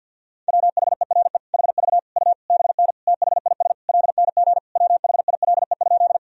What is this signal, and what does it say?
Morse code, heard as a keyed tone, about 52 wpm, 700 Hz. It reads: WHERE HV U BN THIS LNG WHILE?
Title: Sentences from Top 300 Words and Abbreviations 50wpm